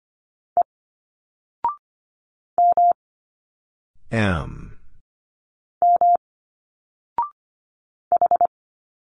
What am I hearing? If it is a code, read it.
Morse code, heard as a keyed tone, about 25 wpm, 700 Hz. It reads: E M M H
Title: Single Letters 25wpm